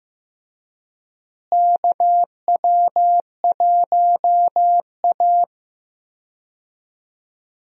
Morse code, heard as a keyed tone, about 15 wpm, 700 Hz. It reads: KW1A
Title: USA Call Signs - 4 Characters - Repeated 2x - 15wpm